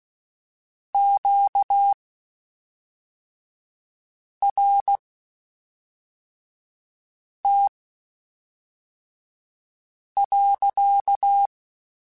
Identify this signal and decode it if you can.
Morse code, heard as a keyed tone, about 16 wpm, spaced as for 3 wpm, 800 Hz. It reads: QRT.
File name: Q Calls / Q Calls-mp3s / Q Calls sppc @5